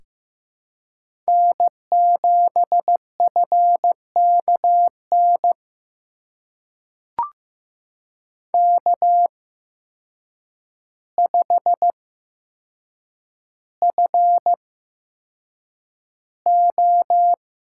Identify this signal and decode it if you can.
Morse code, heard as a keyed tone, about 15 wpm, spaced as for 7 wpm, 700 Hz. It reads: N7FKN K5FO
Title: USA Call Signs - 5 Characters - 8x Character Spacing - 15wpm